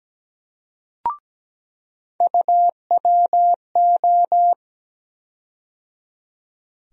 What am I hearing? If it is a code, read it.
Morse code, heard as a keyed tone, about 17 wpm, 700 Hz. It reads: UWO
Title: USA Call Sign Postfixes - 3 Letters 17wpm